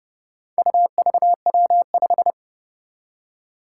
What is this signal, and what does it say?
Morse code, heard as a keyed tone, about 30 wpm, 700 Hz. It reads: UVW5
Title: Four Challenging Letter-Number Combinations 30wpm